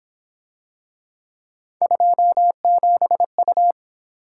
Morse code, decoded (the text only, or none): 27U